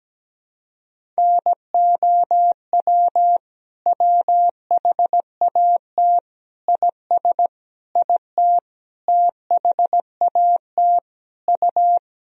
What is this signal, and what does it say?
Morse code, heard as a keyed tone, about 17 wpm, 700 Hz. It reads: NOW WHAT IS IT THAT U
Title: Sentences from Top 200 Words and Abbreviations 17wpm